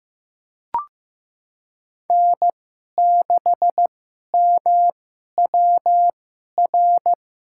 Morse code, heard as a keyed tone, about 15 wpm, spaced as for 11 wpm, 700 Hz. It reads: N6MWR
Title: USA Call Signs - 5 Characters - 2x Character Spacing - 15wpm